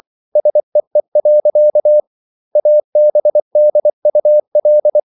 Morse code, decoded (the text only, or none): SEE. ABDUL